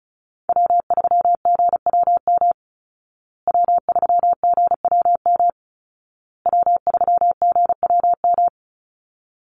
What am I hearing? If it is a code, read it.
Morse code, heard as a keyed tone, about 34 wpm, 700 Hz. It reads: W3GWM W3GWM W3GWM
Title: Call Signs Repeated 2x - 35wpm